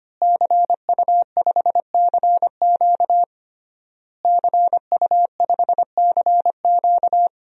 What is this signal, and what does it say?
Morse code, heard as a keyed tone, about 25 wpm, 700 Hz. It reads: CU5CQ CU5CQ
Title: DXCC-100 Call Signs - 5 Characters - Repeated 3x - 25wpm